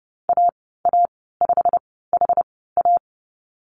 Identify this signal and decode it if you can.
Morse code, heard as a keyed tone, about 31 wpm, spaced as for 17 wpm, 700 Hz. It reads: AA5HA
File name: Lesson 19g - Letter w - Call Signs 30wpm